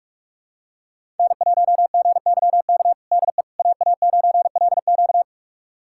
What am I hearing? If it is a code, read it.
Morse code, heard as a keyed tone, about 42 wpm, 700 Hz. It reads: N1GYK DE AA9LQ
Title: USA Callsign DE Callsign - 5 Characters - 45wpm